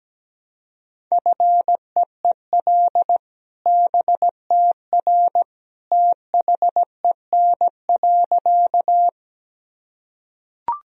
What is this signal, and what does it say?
Morse code, heard as a keyed tone, about 17 wpm, 700 Hz. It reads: FEEL BTR THEN.
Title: Sentences from Top 400 Words and Abbreviations 17wpm